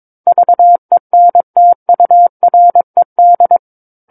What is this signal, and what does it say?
Morse code, heard as a keyed tone, about 22 wpm, 700 Hz. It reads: VENTURED
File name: Book One / 13-Chapter 13 - 22 WPM